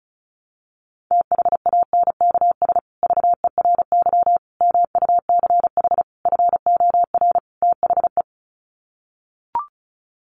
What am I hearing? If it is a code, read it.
Morse code, heard as a keyed tone, about 34 wpm, 700 Hz. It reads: THANKS VERY MUCH FOR THE
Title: Sets of 5 Words Encore2 35wpm